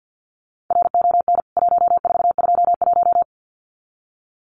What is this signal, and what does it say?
Morse code, heard as a keyed tone, about 50 wpm, 700 Hz. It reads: RON 1421
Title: CWT Exchange 50wpm